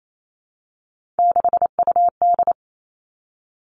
Morse code, heard as a keyed tone, about 28 wpm, 700 Hz. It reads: 6UD
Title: Three Challenging Letter-Number Combinations 28wpm